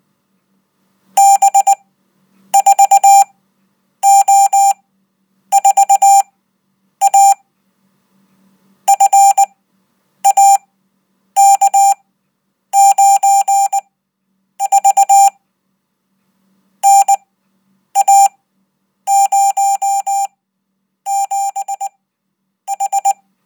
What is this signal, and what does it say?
Morse code, heard as a keyed tone, about 19 wpm, spaced as for 8 wpm, 790 Hz. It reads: B4O4A FAK9V NA07H